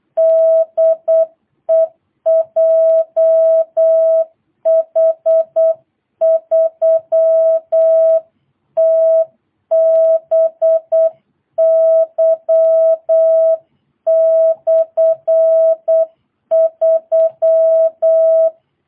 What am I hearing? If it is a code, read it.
Morse code, heard as a keyed tone, about 8 wpm, 650 Hz. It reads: DE JH3TBY/3
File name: jh8cbh / 2022cw